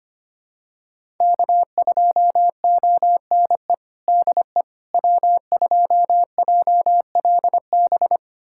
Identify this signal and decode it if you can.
Morse code, heard as a keyed tone, about 25 wpm, 700 Hz. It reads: K2ONE DE W2JLB
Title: USA Callsign DE Callsign - 5 Characters - 25wpm